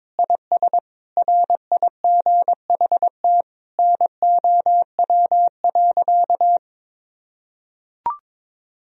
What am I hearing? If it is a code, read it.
Morse code, heard as a keyed tone, about 22 wpm, 700 Hz. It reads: IS RIGHT NOW.
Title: Sentences from Top 1000 Words - Review Part 4 - 22wpm